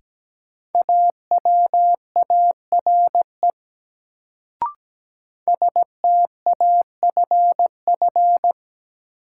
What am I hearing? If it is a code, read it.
Morse code, heard as a keyed tone, about 17 wpm, 700 Hz. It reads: AWARE STAFF
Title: Sentences from Top 1200 Words - New Word Drill - 17wpm